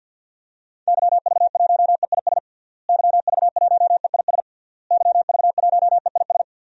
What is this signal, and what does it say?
Morse code, heard as a keyed tone, about 51 wpm, 700 Hz. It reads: YV1EIS YV1EIS YV1EIS
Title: DXCC-100 Call Signs - 6 Characters - Repeated 3x - 50wpm